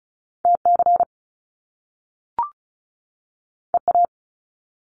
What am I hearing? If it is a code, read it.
Morse code, heard as a keyed tone, about 36 wpm, 700 Hz. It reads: TC EA
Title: Top Two Letter Combinations in Words 35wpm